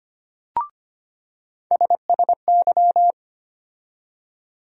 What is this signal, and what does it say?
Morse code, heard as a keyed tone, about 25 wpm, 700 Hz. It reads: SSY
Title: USA Call Sign Postfixes - 3 Letters 25wpm